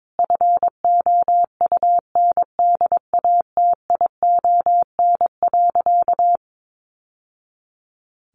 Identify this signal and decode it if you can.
Morse code, heard as a keyed tone, about 22 wpm, 700 Hz. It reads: FOUNDATION.
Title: Sentences from Top 1200 Words 22wpm